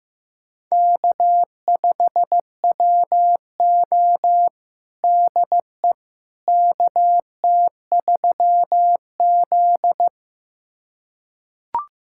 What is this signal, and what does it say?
Morse code, heard as a keyed tone, about 15 wpm, 700 Hz. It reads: K5WO DE KT3Z
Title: USA Callsign DE Callsign - 4 Characters - 15wpm